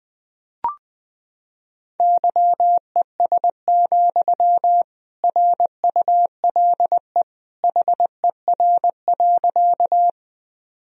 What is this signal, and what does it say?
Morse code, heard as a keyed tone, about 21 wpm, 700 Hz. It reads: YES, RULE HER.
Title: Sentences from Top 1000 Words - Review Part 2 - 20wpm